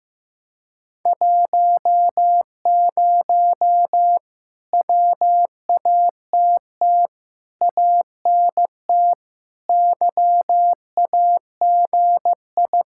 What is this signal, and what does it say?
Morse code, heard as a keyed tone, about 15 wpm, 700 Hz. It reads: 10 WATT ANT YAGI
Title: Speed-Racing Common QSO Phrases Encore - 15wpm